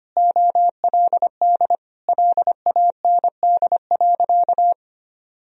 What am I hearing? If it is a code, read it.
Morse code, heard as a keyed tone, about 25 wpm, 700 Hz. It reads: OLD LAND.
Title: Sentences from Top 1000 Words - Review Part 1 - 25wpm